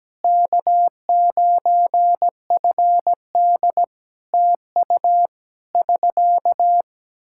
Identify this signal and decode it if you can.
Morse code, heard as a keyed tone, about 17 wpm, 700 Hz. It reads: K9FD TU <SK>
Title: Practice QSO - Exchange 4 - Ending - 17wpm